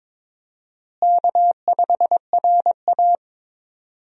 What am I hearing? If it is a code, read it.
Morse code, heard as a keyed tone, about 22 wpm, 700 Hz. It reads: K5RA
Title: USA Call Signs - 4 Characters - Repeated 2x - 22wpm